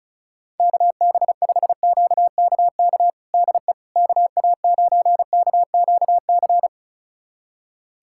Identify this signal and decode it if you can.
Morse code, heard as a keyed tone, about 36 wpm, 700 Hz. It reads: KB5QKK DE KA9KQC
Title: USA Callsign DE Callsign - 6 Characters - 35wpm